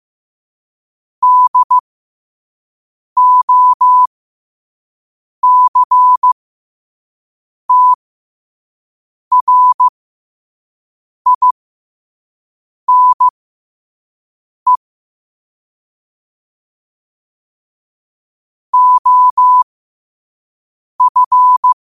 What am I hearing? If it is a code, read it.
Morse code, heard as a keyed tone, about 15 wpm, spaced as for 5 wpm, 1000 Hz. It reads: DOCTRINE OF
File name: qotdfiles / 5wpm